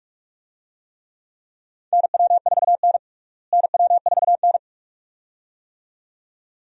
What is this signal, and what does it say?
Morse code, heard as a keyed tone, about 45 wpm, 700 Hz. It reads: NW4N NW4N
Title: USA Call Signs - 4 Characters - Repeated 2x - 45wpm